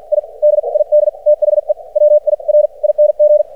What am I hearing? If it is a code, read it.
Morse code, heard as a keyed tone, about 53 wpm, spaced as for 42 wpm, 600 Hz. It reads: I DID THE WIRING
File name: archived / CRSnet